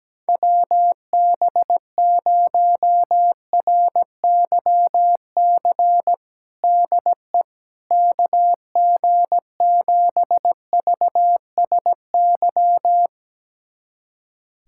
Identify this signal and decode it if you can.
Morse code, heard as a keyed tone, about 17 wpm, 700 Hz. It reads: WB0RYC DE KG7VSY
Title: USA Callsign DE Callsign 17wpm